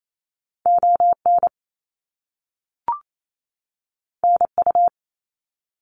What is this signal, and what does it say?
Morse code, heard as a keyed tone, about 28 wpm, 700 Hz. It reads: ON NU
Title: Canadian Provinces and Territories 28wpm